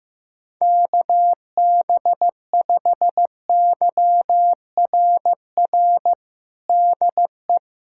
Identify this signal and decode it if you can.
Morse code, heard as a keyed tone, about 15 wpm, 700 Hz. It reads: KB5YRR DE
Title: USA Callsign DE Callsign 15wpm